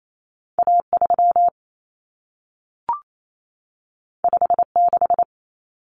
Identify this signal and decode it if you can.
Morse code, heard as a keyed tone, about 28 wpm, 700 Hz. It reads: A3 56
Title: Common Cut Numbers - 2-Digit 28wpm